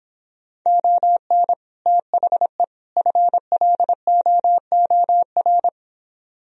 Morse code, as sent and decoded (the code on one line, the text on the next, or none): --- -. / - .... / ..-. .-.. --- --- .-.
ON THE FLOOR